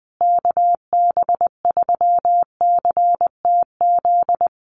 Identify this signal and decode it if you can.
Morse code, heard as a keyed tone, about 20 wpm, 700 Hz. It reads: KB3CTZ
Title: USA Call Signs Repeated 3x 20wpm